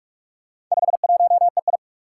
Morse code, heard as a keyed tone, about 41 wpm, 700 Hz. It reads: 51EI
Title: Four Challenging Letter-Number Combinations 45wpm